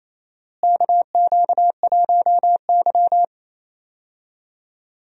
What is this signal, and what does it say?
Morse code, heard as a keyed tone, about 28 wpm, 700 Hz. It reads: KQ1Y
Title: USA Call Signs - 4 Characters - Repeated 2x - 28wpm